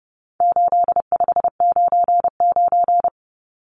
Morse code, heard as a keyed tone, about 30 wpm, 700 Hz. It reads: 8599
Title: Numbers - 4-Digit 30wpm